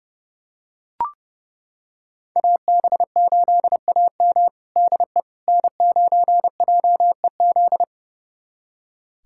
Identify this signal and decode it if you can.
Morse code, heard as a keyed tone, about 30 wpm, 700 Hz. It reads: AB8AM DE N9JEZ
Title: USA Callsign DE Callsign - 5 Characters - 30wpm